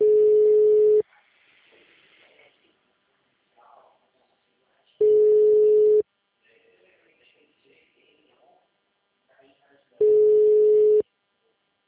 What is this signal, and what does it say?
Ringback tone (Europe, 425 Hz), with speech.